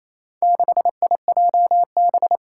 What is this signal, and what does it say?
Morse code, heard as a keyed tone, about 27 wpm, 700 Hz. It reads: 6IJB
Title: Four Challenging Letter-Number Combinations 28wpm